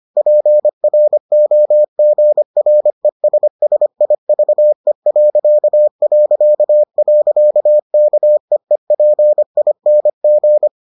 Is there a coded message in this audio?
Yes: Morse code from 0.2 to 10.7 s, heard as a keyed tone, about 25 wpm, 600 Hz. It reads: PROGRESSIVE...KEEPING